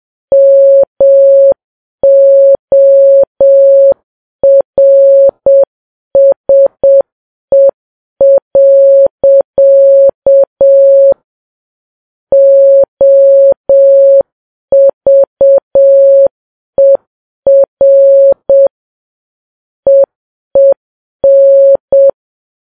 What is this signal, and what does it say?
Morse code, heard as a keyed tone, about 7 wpm, 550 Hz. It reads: MORSE. OVER EEN